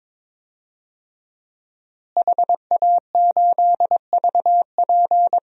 Morse code, heard as a keyed tone, about 22 wpm, 700 Hz. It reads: HA8VP
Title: DXCC-100 Call Signs - 5 Characters - Repeated 3x - 22wpm